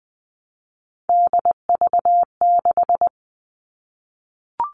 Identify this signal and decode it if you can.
Morse code, heard as a keyed tone, about 20 wpm, 700 Hz. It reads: DV6
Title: All Cut Numbers - 3-Digit 20wpm